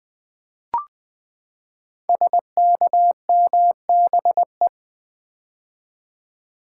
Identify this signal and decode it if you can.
Morse code, heard as a keyed tone, about 20 wpm, 700 Hz. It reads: SKMBE
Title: Code Groups - 5 Letters - 20wpm